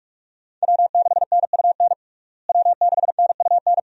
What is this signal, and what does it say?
Morse code, heard as a keyed tone, about 46 wpm, 700 Hz. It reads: W6NUN W6NUN
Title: USA Call Signs - 5 Characters - Repeated 2x - 45wpm